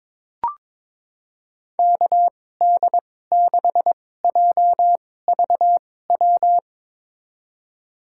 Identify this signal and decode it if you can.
Morse code, heard as a keyed tone, about 22 wpm, spaced as for 16 wpm, 700 Hz. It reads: KD6JVW
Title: USA Call Signs - 2x Character Spacing - 22wpm